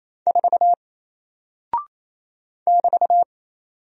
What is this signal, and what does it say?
Morse code, heard as a keyed tone, about 28 wpm, 700 Hz. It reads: 4 <BT>
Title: Single Character (Letter:Number:Punctuation:Prosign) Alt 28wpm